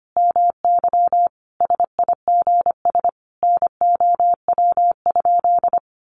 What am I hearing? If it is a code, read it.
Morse code, heard as a keyed tone, about 25 wpm, 700 Hz. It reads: MY SIGS NOW?
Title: Common QSO Phrases 25wpm